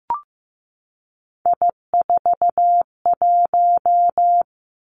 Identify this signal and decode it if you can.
Morse code, heard as a keyed tone, about 15 wpm, 700 Hz. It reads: I41
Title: Three Challenging Letter-Number Combinations 15wpm